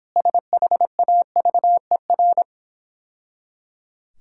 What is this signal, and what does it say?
Morse code, heard as a keyed tone, about 26 wpm, 700 Hz. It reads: SHAVER